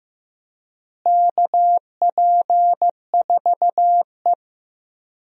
Morse code, heard as a keyed tone, about 15 wpm, 700 Hz. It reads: KP4E
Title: DXCC-100 Call Signs - 4 Characters - Repeated 3x - 15wpm